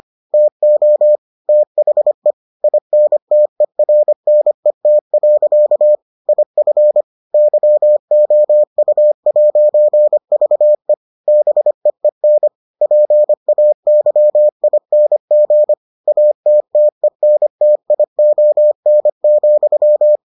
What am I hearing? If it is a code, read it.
Morse code, heard as a keyed tone, about 25 wpm, 600 Hz. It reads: TO THE INTERNET. IF YOU'VE BEEN PAYING ATTENTION,